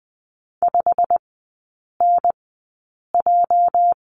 Morse code, heard as a keyed tone, about 20 wpm, spaced as for 8 wpm, 690 Hz. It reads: 5NJ
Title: 205 Lesson 33e - Letter j - Three Characters 20wpm